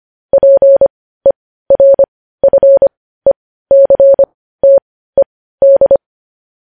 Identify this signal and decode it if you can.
Morse code, heard as a keyed tone, about 25 wpm, spaced as for 15 wpm, 550 Hz. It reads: PERFECTED
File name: podcast / News headlines-1x-25x15